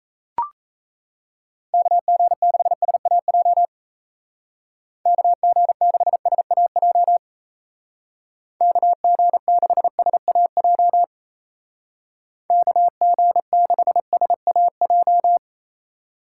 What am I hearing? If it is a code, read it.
Morse code, heard as a keyed tone, about 34 wpm, 700 Hz. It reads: KG6SAJ KG6SAJ KG6SAJ KG6SAJ
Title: Speed Racing - USA 2x3 Call Signs 28wpm